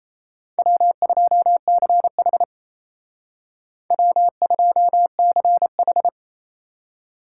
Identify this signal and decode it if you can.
Morse code, heard as a keyed tone, about 32 wpm, 700 Hz. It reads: W2CH W2CH
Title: Speed Racing - USA 1x2 Call Signs 28wpm